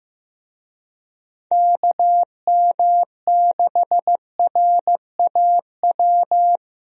Morse code, heard as a keyed tone, about 15 wpm, 700 Hz. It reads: KM6RAW
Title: USA Call Signs - 6 Characters - Repeated 3x - 15wpm